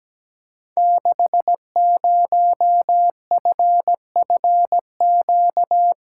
Morse code, heard as a keyed tone, about 17 wpm, 700 Hz. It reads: 60FFQ
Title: Code Groups - 5 Letter:Numbers - 17wpm